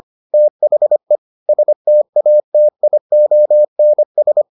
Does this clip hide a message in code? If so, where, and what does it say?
Yes: Morse code from 0.3 to 4.4 s, heard as a keyed tone, about 25 wpm, 600 Hz. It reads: THE STATIONS